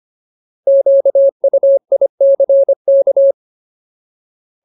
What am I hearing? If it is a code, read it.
Morse code, heard as a keyed tone, about 25 wpm, 550 Hz. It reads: QUICK
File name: Ultimate Word List / 25WPM Ultimate